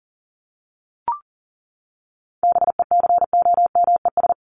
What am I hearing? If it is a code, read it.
Morse code, heard as a keyed tone, about 39 wpm, 700 Hz. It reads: BECOMES
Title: ES:S Suffix - Top 100 Words with Base Words 40wpm